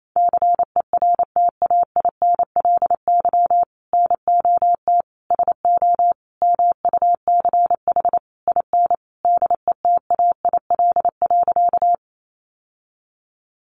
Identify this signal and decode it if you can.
Morse code, heard as a keyed tone, about 28 wpm, 700 Hz. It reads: CERTAINLY NOT SO MUCH IN DETAIL.
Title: Sentences from Top 1500 Words 28wpm